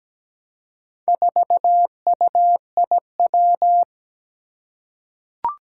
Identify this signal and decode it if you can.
Morse code, heard as a keyed tone, about 17 wpm, 700 Hz. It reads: 4UIW
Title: Four Challenging Letter-Number Combinations 17wpm